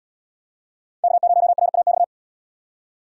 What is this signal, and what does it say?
Morse code, heard as a keyed tone, about 68 wpm, 700 Hz. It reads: K0RAX